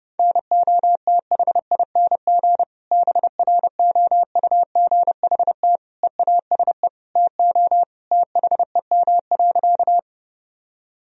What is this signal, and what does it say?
Morse code, heard as a keyed tone, about 30 wpm, 700 Hz. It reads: NOTHING BROUGHT EASE TO THEM.
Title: Sentences from Top 1000 Words - Review Part 1 - 30wpm